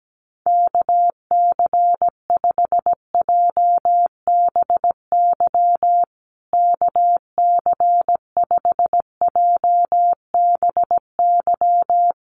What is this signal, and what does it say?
Morse code, heard as a keyed tone, about 17 wpm, 700 Hz. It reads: KC5JBY KC5JBY
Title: USA Call Signs Repeated 3x 17wpm